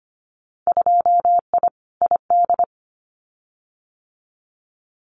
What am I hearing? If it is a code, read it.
Morse code, heard as a keyed tone, about 25 wpm, 700 Hz. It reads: 2I ID
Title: Winter Field Day Exchanges Encore 25wpm